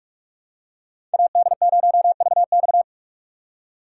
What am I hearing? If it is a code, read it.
Morse code, heard as a keyed tone, about 46 wpm, 700 Hz. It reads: AD0VX
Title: USA Call Signs - 5 Characters - 3x Character Spacing - 45wpm